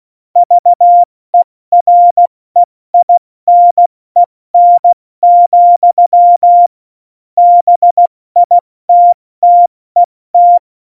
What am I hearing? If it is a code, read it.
Morse code, heard as a keyed tone, about 16 wpm, 700 Hz. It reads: VEREINEN, BITTET
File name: Morsekurs / Witze / Witz 27. August 2019, 70 F80